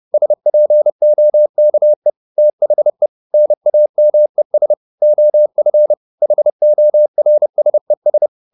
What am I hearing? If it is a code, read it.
Morse code, heard as a keyed tone, about 30 wpm, 600 Hz. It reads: SPOKE THE NAMES OF HORSES
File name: the old man and the sea / the old man and the sea 1 30wpm